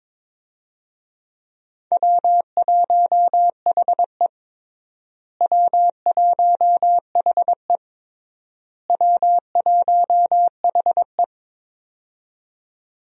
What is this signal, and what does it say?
Morse code, heard as a keyed tone, about 21 wpm, 700 Hz. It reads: W1HE W1HE W1HE
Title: USA Call Signs - 4 Characters - Repeated 3x - 22wpm